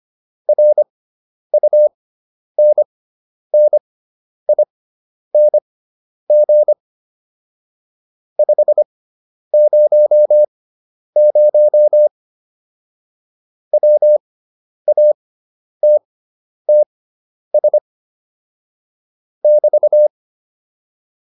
Morse code, heard as a keyed tone, about 25 wpm, spaced as for 10 wpm, 600 Hz. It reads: RUNNING 500 WATTS <BT>